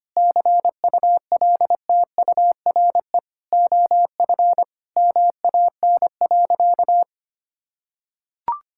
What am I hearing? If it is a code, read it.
Morse code, heard as a keyed tone, about 25 wpm, 700 Hz. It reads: CULTURE OF MAN.
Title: Sentences from Top 1000 Words - Review Part 6 - 25wpm